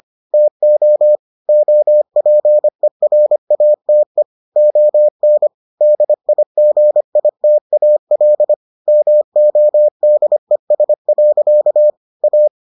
Morse code, heard as a keyed tone, about 25 wpm, 600 Hz. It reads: TO OPERATE ON DIGITAL MODES. A